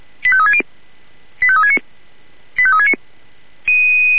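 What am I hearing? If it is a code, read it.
Five-tone selective call (ZVEI), heard as quick runs of five tones, tones of about 70 ms. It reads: 85360 85369 85369
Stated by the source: ZVEI five-tone selective call received by SDR.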